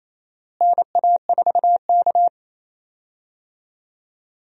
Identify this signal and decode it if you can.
Morse code, heard as a keyed tone, about 28 wpm, 700 Hz. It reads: NA4K
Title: USA Call Signs - 4 Characters - Repeated 2x - 28wpm